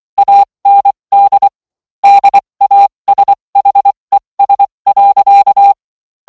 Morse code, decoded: AND DASHES.